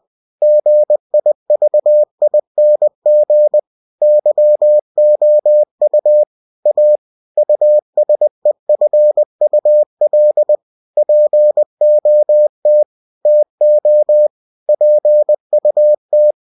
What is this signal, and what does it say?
Morse code, heard as a keyed tone, about 20 wpm, 600 Hz. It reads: GIVING YOU A USEFUL POT TO PUT
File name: cw-books / winnie-the-pooh / winnie-20wpm-page-0058